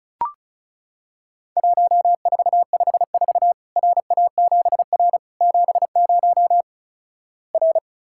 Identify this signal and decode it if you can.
Morse code, heard as a keyed tone, about 34 wpm, 700 Hz. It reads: 1454 RA7R 70 R
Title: Worked All Europe DX Contest - QTC Exchange - 35wpm